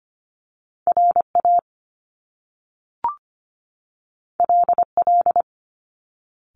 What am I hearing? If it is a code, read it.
Morse code, heard as a keyed tone, about 25 wpm, 700 Hz. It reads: RA LL
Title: Top Two Letter Combinations in Words 25wpm